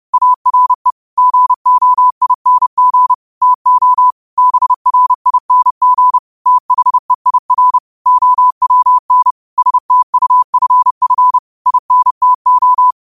Morse code, heard as a keyed tone, about 30 wpm, 1000 Hz. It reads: ARE GOING TO BRING THEIR OWN STUFF INTO